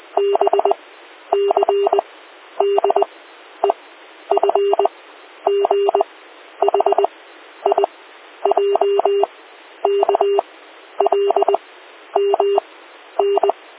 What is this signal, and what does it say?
Morse code, heard as a keyed tone, about 20 wpm, 400 Hz. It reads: B C D E F G H I J K L M N